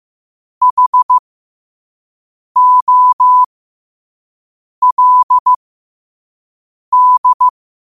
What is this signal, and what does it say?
Morse code, heard as a keyed tone, about 15 wpm, spaced as for 5 wpm, 1000 Hz. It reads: HOLD
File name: qotdfiles / 5wpm